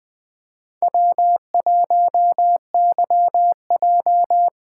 Morse code, heard as a keyed tone, about 20 wpm, 700 Hz. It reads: W1YJ